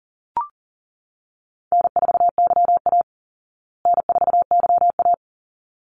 Morse code, heard as a keyed tone, about 40 wpm, 700 Hz. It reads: N4YA N4YA
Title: Easy Call Signs Repeated 2x - 40wpm